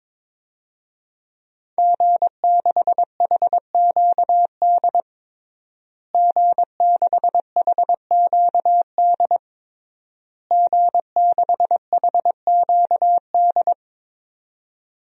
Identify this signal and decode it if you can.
Morse code, heard as a keyed tone, about 22 wpm, 700 Hz. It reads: G6HQD G6HQD G6HQD
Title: DXCC-100 Call Signs - 5 Characters - Repeated 3x - 22wpm